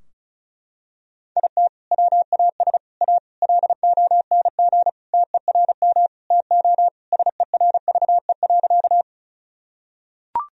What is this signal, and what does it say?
Morse code, heard as a keyed tone, about 34 wpm, 700 Hz. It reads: IT WAS A LONG TERM TO SERVE.
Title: Sentences from Top 1000 Words - Review Part 4 - 35wpm